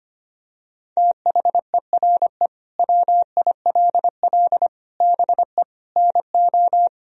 Morse code, heard as a keyed tone, about 25 wpm, 700 Hz. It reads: THERE WILL BE NO
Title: Sentences from Top 1000 Words - Review Part 4 - 25wpm